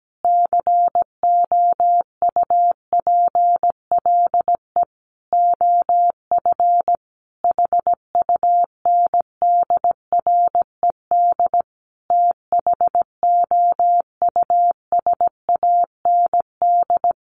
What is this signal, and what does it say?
Morse code, heard as a keyed tone, about 17 wpm, 700 Hz. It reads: COUPLE OF HUNDRED THOUSAND
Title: Sentences from Top 1100 Words 17wpm